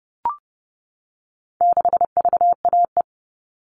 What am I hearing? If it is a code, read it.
Morse code, heard as a keyed tone, about 30 wpm, 700 Hz. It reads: 6VAE
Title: All Cut Numbers - 4-Digit 30wpm